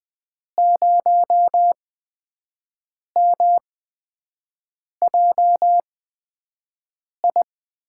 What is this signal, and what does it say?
Morse code, heard as a keyed tone, about 19 wpm, spaced as for 5 wpm, 700 Hz. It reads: 0MJI
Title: USA Call Signs - 6 Characters - 8x Character Spacing - 20wpm